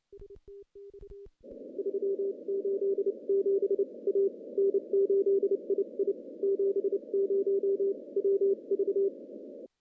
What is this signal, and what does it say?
Morse code, heard as a keyed tone, about 31 wpm, spaced as for 27 wpm, 400 Hz. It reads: 387 A N8II 70 WV
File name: contests / CW-SS-k5tr- -1500